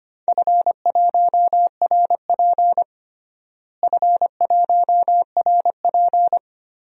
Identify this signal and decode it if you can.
Morse code, heard as a keyed tone, about 25 wpm, 700 Hz. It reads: F1RP F1RP
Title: DXCC-100 Call Signs - 4 Characters - Repeated 3x - 25wpm